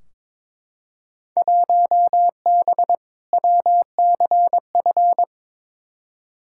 Morse code, heard as a keyed tone, about 22 wpm, 700 Hz. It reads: 1B WCF
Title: Field Day Exchanges Encore - 22wpm